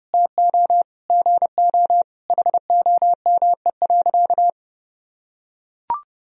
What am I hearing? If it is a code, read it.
Morse code, heard as a keyed tone, about 30 wpm, 700 Hz. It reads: TO GO HOME.
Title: Sentences from Top 1000 Words - Review Part 4 - 30wpm